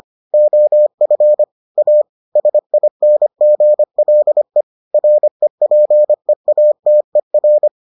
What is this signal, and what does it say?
Morse code, heard as a keyed tone, about 25 wpm, 600 Hz. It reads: OF A SINGLE REPEATER